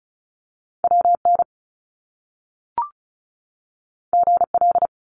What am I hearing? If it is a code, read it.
Morse code, heard as a keyed tone, about 36 wpm, 700 Hz. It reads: WN GL
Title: Top Two Letter Combinations in Words 35wpm